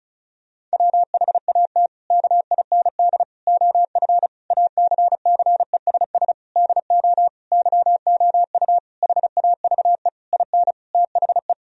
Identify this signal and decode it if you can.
Morse code, heard as a keyed tone, about 34 wpm, 700 Hz. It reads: WHAT KIND OF ACCESS DO YOU HAVE IN THE
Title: Sentences from Top 1000 Words - Review Part 4 - 35wpm